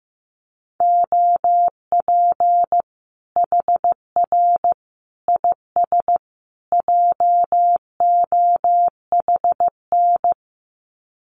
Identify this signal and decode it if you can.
Morse code, heard as a keyed tone, about 15 wpm, 700 Hz. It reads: OP HR IS JOHN
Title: Common QSO Phrases 15wpm